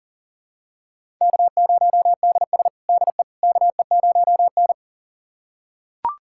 Morse code, heard as a keyed tone, about 40 wpm, 700 Hz. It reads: K0DS DE KE0N
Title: USA Callsign DE Callsign - 4 Characters - 40wpm